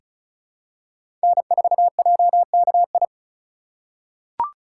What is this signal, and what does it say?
Morse code, heard as a keyed tone, about 34 wpm, 700 Hz. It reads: N4JKI
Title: USA Call Signs - 4x Character Spacing - 35wpm